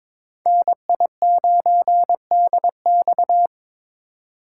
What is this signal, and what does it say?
Morse code, heard as a keyed tone, about 22 wpm, 700 Hz. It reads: NI9DX
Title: USA Call Signs - 5 Characters - 3x Character Spacing - 22wpm